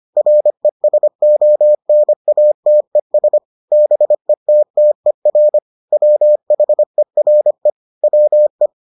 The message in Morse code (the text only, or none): RESONATES BETTER WHERE WE